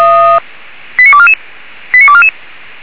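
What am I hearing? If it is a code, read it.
Five-tone selective call (ZVEI), heard as quick runs of five tones, tones of about 70 ms. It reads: 89244 89244
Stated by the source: ZVEI five-tone call 89243 received by SDR.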